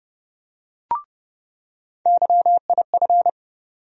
Morse code, heard as a keyed tone, about 30 wpm, 700 Hz. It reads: YIF
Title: USA Call Sign Postfixes - 3 Letters 30wpm